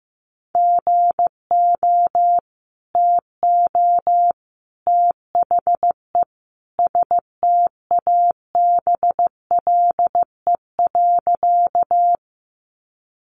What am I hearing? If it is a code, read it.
Morse code, heard as a keyed tone, about 14 wpm, 700 Hz. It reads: GO TO THE STABLE.
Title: Sentences from Top 1500 Words 15wpm